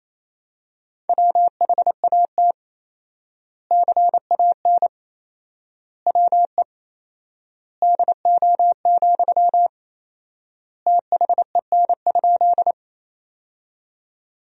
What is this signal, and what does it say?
Morse code, heard as a keyed tone, about 28 wpm, 700 Hz. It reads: WHAT CAN WE DO, THEN?
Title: 4x Word-Spacing - Sentences from Top 100 Words - 28wpm